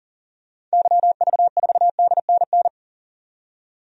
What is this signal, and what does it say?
Morse code, heard as a keyed tone, about 40 wpm, 700 Hz. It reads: YV4DNN
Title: DXCC-100 Call Signs - 6 Characters - 40wpm